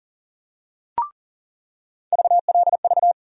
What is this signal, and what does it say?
Morse code, heard as a keyed tone, about 40 wpm, 700 Hz. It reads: VLV
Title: USA Call Sign Postfixes - 3 Letters 40wpm